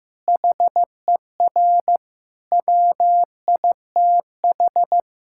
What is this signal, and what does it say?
Morse code, heard as a keyed tone, about 15 wpm, 700 Hz. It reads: HER WITH
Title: Sentences from Top 1600 Words 15wpm